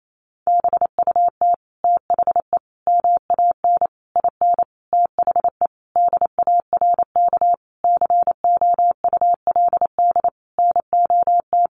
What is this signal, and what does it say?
Morse code, heard as a keyed tone, about 28 wpm, 700 Hz. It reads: BUT THE MAN IN THE DARK COULD NOT
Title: Sentences from Top 600 Words 28wpm